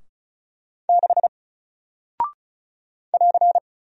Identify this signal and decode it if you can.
Morse code, heard as a keyed tone, about 36 wpm, 700 Hz. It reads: 6 <AR>
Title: Single Character (Letter:Number:Punctuation:Prosign) Alt 35wpm